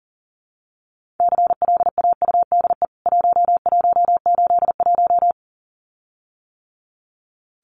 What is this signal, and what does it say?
Morse code, heard as a keyed tone, about 40 wpm, 700 Hz. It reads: CLAUDE 1181
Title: CWT Exchange 40wpm